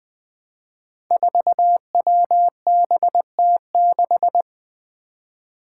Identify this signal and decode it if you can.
Morse code, heard as a keyed tone, about 20 wpm, 700 Hz. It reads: 4WBT6
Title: Five Challenging Letter-Number Combinations 20wpm